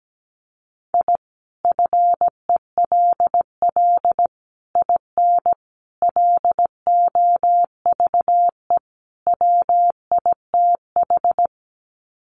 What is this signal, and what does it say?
Morse code, heard as a keyed tone, about 17 wpm, 700 Hz. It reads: I FELL IN LOVE WITH
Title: Sets of 5 Words Encore 17wpm